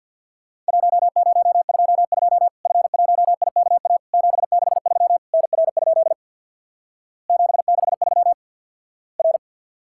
Morse code, heard as a keyed tone, about 50 wpm, 700 Hz. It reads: 1022 F1IKA 763 NR? 763 R
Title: Worked All Europe DX Contest - QTC Exchange with Repeats - 50wpm